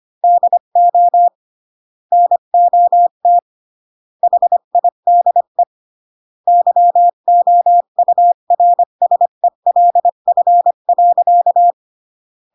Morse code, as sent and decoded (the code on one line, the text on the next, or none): -.. --- / -. --- - / .... .. -.. / -.-- --- ..- .-. ... .-.. ..-. .-.-.-
DO NOT HIDE YOURSELF.